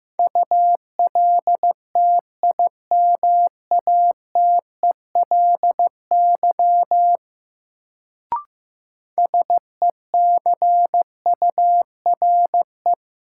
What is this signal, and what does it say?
Morse code, heard as a keyed tone, about 15 wpm, 700 Hz. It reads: ULTIMATELY SECURE
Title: Sentences from Top 1200 Words - New Word Drill Alt - 15wpm